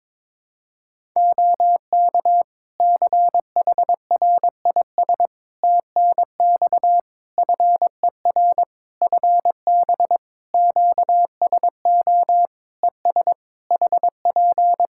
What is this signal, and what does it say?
Morse code, heard as a keyed tone, about 22 wpm, 700 Hz. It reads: OK CHRIS TNX FER FB QSO ES HP
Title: Practice QSO - Exchange 4 - Ending - 22wpm